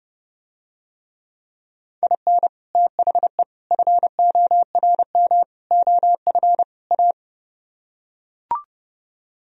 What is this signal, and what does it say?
Morse code, heard as a keyed tone, about 32 wpm, 700 Hz. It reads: IN THE FORM OF A